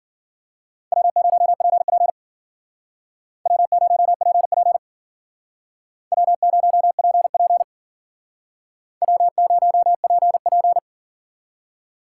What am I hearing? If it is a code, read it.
Morse code, heard as a keyed tone, about 52 wpm, 700 Hz. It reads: W0PP W0PP W0PP W0PP